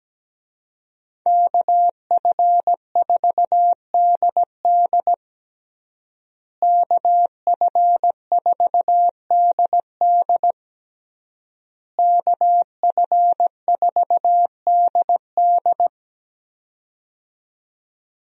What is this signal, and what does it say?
Morse code, heard as a keyed tone, about 17 wpm, 700 Hz. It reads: KF4DD KF4DD KF4DD
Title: USA Call Signs - 5 Characters - Repeated 3x - 17wpm